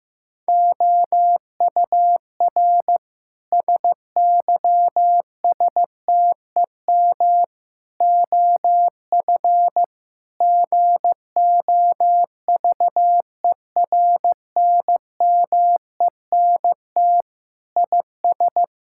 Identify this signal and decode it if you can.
Morse code, heard as a keyed tone, about 15 wpm, 700 Hz. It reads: OUR SYSTEM OF GOVERNMENT IS
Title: Sentences from Top 1000 Words - Review Part 6 - 15wpm